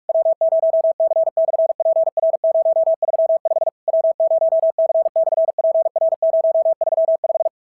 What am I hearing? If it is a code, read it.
Morse code, heard as a keyed tone, about 46 wpm, 690 Hz. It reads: W0C/PR035 W0C/PR035
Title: SOTA QSO Practice 45wpm